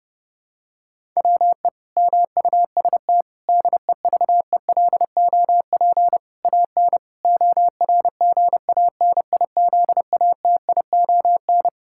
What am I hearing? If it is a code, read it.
Morse code, heard as a keyed tone, about 30 wpm, 700 Hz. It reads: WE MUST DEVELOP AN ORGANIZATION
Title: Sentences from Top 1000 Words - Review Part 5 - 30wpm